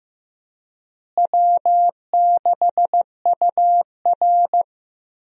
Morse code, heard as a keyed tone, about 15 wpm, 700 Hz. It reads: W6UR
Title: USA Call Signs - 4 Characters - Repeated 3x - 15wpm